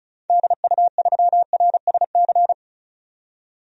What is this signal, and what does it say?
Morse code, heard as a keyed tone, about 36 wpm, 700 Hz. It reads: DU3RSC
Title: DXCC-100 Call Signs - 6 Characters - 35wpm